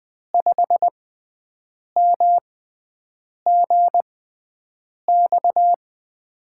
Morse code, heard as a keyed tone, about 21 wpm, spaced as for 7 wpm, 700 Hz. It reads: 5MGX
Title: USA Call Signs - 6x Character Spacing - 20wpm